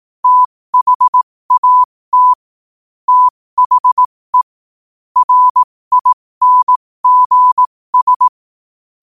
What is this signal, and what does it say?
Morse code, heard as a keyed tone, about 18 wpm, spaced as for 15 wpm, 1000 Hz. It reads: THAT THE RINGS
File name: qotdfiles / 15wpm